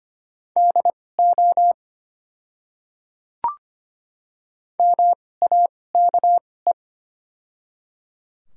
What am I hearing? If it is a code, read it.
Morse code, heard as a keyed tone, about 25 wpm, spaced as for 18 wpm, 700 Hz. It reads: DO MAKE
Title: 2x Character Spacing - Top 100 Words - 25wpm